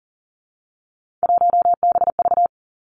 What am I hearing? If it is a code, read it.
Morse code, heard as a keyed tone, about 39 wpm, 700 Hz. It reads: 1BV
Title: Three Challenging Letter-Number Combinations 40wpm